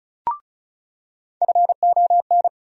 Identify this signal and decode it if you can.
Morse code, heard as a keyed tone, about 36 wpm, 700 Hz. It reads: FON
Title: USA Call Sign Postfixes - 3 Letters 35wpm